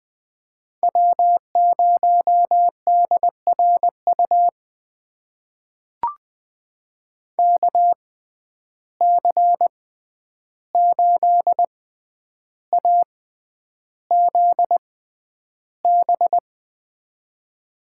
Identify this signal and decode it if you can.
Morse code, heard as a keyed tone, about 20 wpm, spaced as for 10 wpm, 700 Hz. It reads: W0DRU KC8AZB
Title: USA Call Signs - 6x Character Spacing - 20wpm